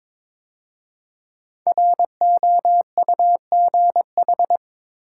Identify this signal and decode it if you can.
Morse code, heard as a keyed tone, about 22 wpm, 700 Hz. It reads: ROUGH
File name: 5 Letter Words - First 3 Letters Given - 22wpm